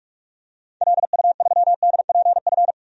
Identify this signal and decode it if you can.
Morse code, heard as a keyed tone, about 46 wpm, 700 Hz. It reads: LU3DPF
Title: DXCC-100 Call Signs - 6 Characters - 45wpm